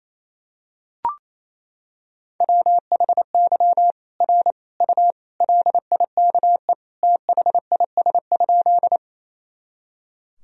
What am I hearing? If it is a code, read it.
Morse code, heard as a keyed tone, about 28 wpm, 700 Hz. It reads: WHY R U LIKE THIS?
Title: Sentences from Top 200 Words and Abbreviations 28wpm